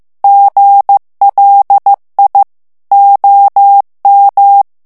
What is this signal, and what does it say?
Morse code, heard as a keyed tone, about 15 wpm, 800 Hz. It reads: GLI OM
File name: morse mp3 / ITALIEN texte clair 2 vitesse 14 mots minute